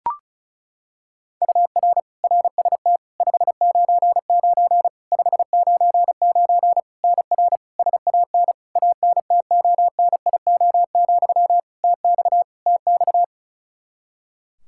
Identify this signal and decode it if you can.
Morse code, heard as a keyed tone, about 34 wpm, 700 Hz. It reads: UR RST 599 599 NR SAN ANTONIO, TX TX
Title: K1USN SST QSO Practice 35wpm